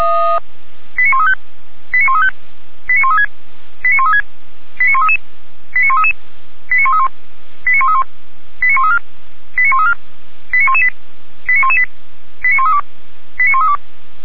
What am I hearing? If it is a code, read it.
Five-tone selective call (ZVEI), heard as quick runs of five tones, tones of about 70 ms. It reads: 89135 89135 89136 89136 89133 89133 89131 89131 89134 89134 89118 89118 89132 89132
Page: SDR/ZVEI